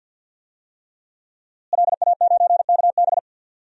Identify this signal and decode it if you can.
Morse code, heard as a keyed tone, about 53 wpm, 700 Hz. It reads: LA9KB
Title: DXCC-100 Call Signs - 5 Characters - Repeated 3x - 50wpm